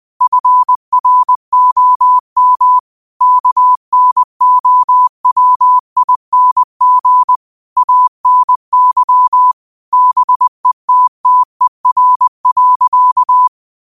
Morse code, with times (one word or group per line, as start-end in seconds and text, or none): 0.2-2.8 FROM
3.2-7.4 KNOWING
7.8-9.5 ANY
9.9-13.5 BETTER.